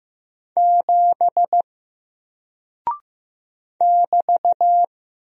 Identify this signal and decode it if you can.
Morse code, heard as a keyed tone, about 15 wpm, 700 Hz. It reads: 7 <BT>
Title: Single Character (Letter:Number:Punctuation:Prosign) 15wpm